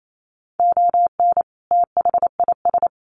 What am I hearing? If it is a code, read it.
Morse code, heard as a keyed tone, about 28 wpm, 700 Hz. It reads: ON THIS
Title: Sets of 2 Words 28wpm